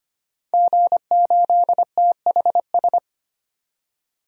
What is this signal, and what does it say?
Morse code, heard as a keyed tone, about 25 wpm, 700 Hz. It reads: G8THS
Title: DXCC-100 Call Signs - 5 Characters - 25wpm